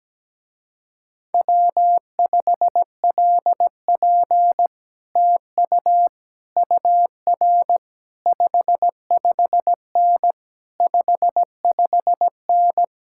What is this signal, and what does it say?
Morse code, heard as a keyed tone, about 17 wpm, 700 Hz. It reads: W5LP TU UR 55N 55N
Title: Speed-Racing Common QSO Phrases Encore - 17wpm